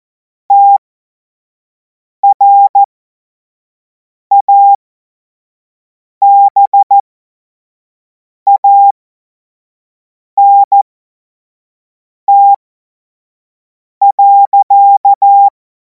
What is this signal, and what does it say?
Morse code, heard as a keyed tone, about 14 wpm, spaced as for 5 wpm, 800 Hz. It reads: TRABANT.
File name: Morsekurs / Witze / Witz 23. Juni 2026 25 F70